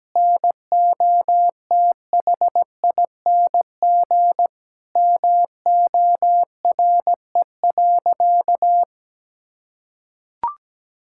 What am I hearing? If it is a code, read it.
Morse code, heard as a keyed tone, about 17 wpm, 700 Hz. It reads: NOTHING MORE.
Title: Sentences from Top 1000 Words - Review Part 5 - 17wpm